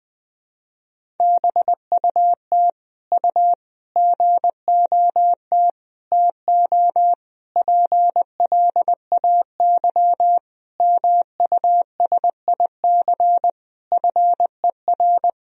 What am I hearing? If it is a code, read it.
Morse code, heard as a keyed tone, about 20 wpm, 700 Hz. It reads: BUT U GOT TO PLAY MUSIC FER
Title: Sentences from Top 300 Words and Abbreviations 20wpm